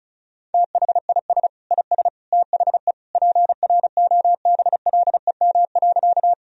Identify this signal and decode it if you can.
Morse code, heard as a keyed tone, about 36 wpm, 700 Hz. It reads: THIS IS THE PROBLEM.
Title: Sentences from Top 1000 Words - Review Part 2 - 35wpm